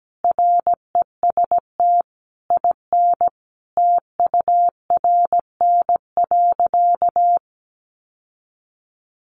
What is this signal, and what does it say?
Morse code, heard as a keyed tone, about 17 wpm, 700 Hz. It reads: REST IN TURN.
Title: Sentences from Top 700 Words 17wpm